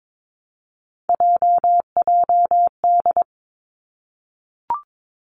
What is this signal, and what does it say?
Morse code, heard as a keyed tone, about 22 wpm, 700 Hz. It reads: JJD
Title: Three Challenging Letter-Number Combinations 22wpm